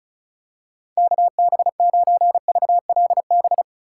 Morse code, heard as a keyed tone, about 36 wpm, 700 Hz. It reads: KB9VLB
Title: USA Call Signs - 6 Characters - 2x Character Spacing - 35wpm